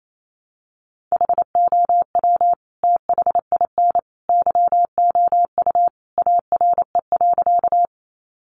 Morse code, heard as a keyed tone, about 28 wpm, 700 Hz. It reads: HOW THIN YOU ARE.
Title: Sentences from Top 1200 Words 28wpm